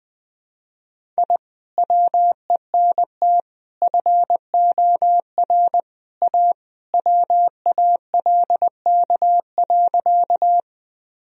Morse code, heard as a keyed tone, about 19 wpm, 700 Hz. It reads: I WENT FOR A WALK.
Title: Sentences from Top 1000 Words - Review Part 1 - 20wpm